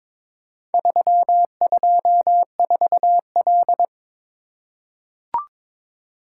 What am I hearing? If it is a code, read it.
Morse code, heard as a keyed tone, about 22 wpm, 700 Hz. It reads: 324L
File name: practice / 22wpm / Four Letter-Number Combinations - 22wpm